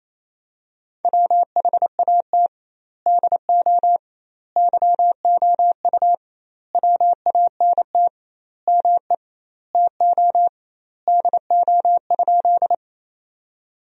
Morse code, heard as a keyed tone, about 28 wpm, 700 Hz. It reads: WHAT DO YOU WANT ME TO DO?
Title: 2x Word-Spacing - Sentences from Top 200 Words - 28wpm